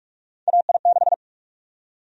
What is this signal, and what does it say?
Morse code, heard as a keyed tone, about 45 wpm, 700 Hz. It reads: AI6
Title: USA Call Sign Prefixes - 45wpm